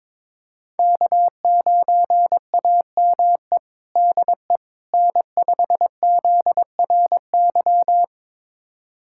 Morse code, heard as a keyed tone, about 22 wpm, 700 Hz. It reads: K9AME DE N5ZRY
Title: USA Callsign DE Callsign - 5 Characters - 22wpm